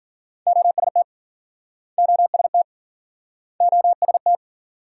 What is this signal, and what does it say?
Morse code, heard as a keyed tone, about 50 wpm, 700 Hz. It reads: OST OST OST
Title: Speed-Racing Multi-letter Phonemes - 40wpm